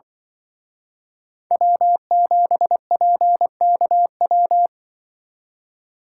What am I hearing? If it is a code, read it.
Morse code, heard as a keyed tone, about 23 wpm, 700 Hz. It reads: W7PKW
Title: Speed Racing - USA 1x3 Call Signs 20wpm